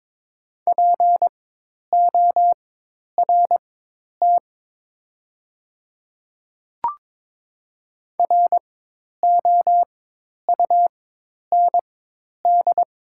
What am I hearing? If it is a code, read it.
Morse code, heard as a keyed tone, about 22 wpm, spaced as for 10 wpm, 700 Hz. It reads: PORT ROUND
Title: 4x Character Spacing - Top 200 Words - 22wpm